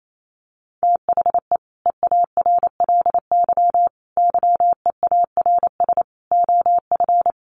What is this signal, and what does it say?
Morse code, heard as a keyed tone, about 28 wpm, 700 Hz. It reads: THE EARLY YEARS OF
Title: Sets of 5 Words Encore 28wpm